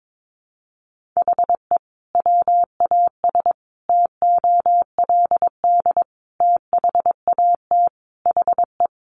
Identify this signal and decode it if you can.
Morse code, heard as a keyed tone, about 22 wpm, 700 Hz. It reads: HE WAS TOLD THAT HE
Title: Sentences from Top 1500 Words 22wpm